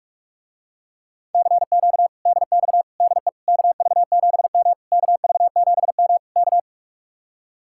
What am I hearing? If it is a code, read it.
Morse code, heard as a keyed tone, about 44 wpm, 700 Hz. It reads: CQ DX DE KV7M KV7M K